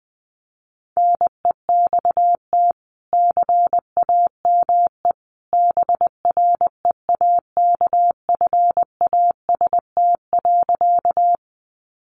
Morse code, heard as a keyed tone, about 20 wpm, 700 Hz. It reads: NEXT CAME BREAKFAST.
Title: Sentences from Top 1500 Words 20wpm